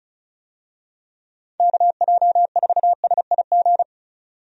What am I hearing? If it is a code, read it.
Morse code, heard as a keyed tone, about 36 wpm, 700 Hz. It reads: KJ4SIG
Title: USA Call Signs - 6 Characters - 35wpm